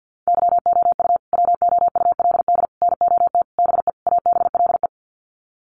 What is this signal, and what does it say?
Morse code, heard as a keyed tone, about 50 wpm, 700 Hz. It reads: YOU WOULD NOT BE ABLE
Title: Sets of 5 Words Encore2 50wpm